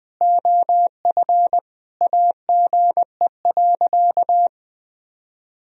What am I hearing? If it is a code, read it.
Morse code, heard as a keyed tone, about 20 wpm, 700 Hz. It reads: OF AGE.
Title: Sentences from Top 1000 Words - Review Part 3 - 20wpm